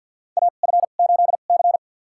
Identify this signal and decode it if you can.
Morse code, heard as a keyed tone, about 50 wpm, spaced as for 36 wpm, 700 Hz. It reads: AF8C
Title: USA Call Signs - 4 Characters - 2x Character Spacing - 50wpm